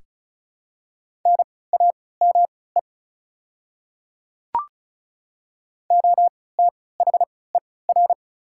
Morse code, heard as a keyed tone, about 36 wpm, spaced as for 20 wpm, 700 Hz. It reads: NAME OTHER
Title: 3x Character Spacing - Top 200 Words - 35wpm